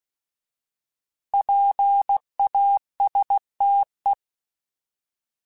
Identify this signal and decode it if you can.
Morse code, heard as a keyed tone, about 16 wpm, 800 Hz. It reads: PASTE
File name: Alphabetical / Alphabetical mp3s / Ps @15